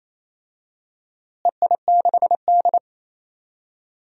Morse code, heard as a keyed tone, about 28 wpm, 700 Hz. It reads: EI6D
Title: DXCC-100 Call Signs - 4 Characters - Repeated 2x - 28wpm